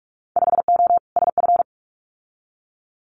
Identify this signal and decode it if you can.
Morse code, heard as a keyed tone, about 46 wpm, 700 Hz. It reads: 5O SF
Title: Winter Field Day Exchanges Encore 45wpm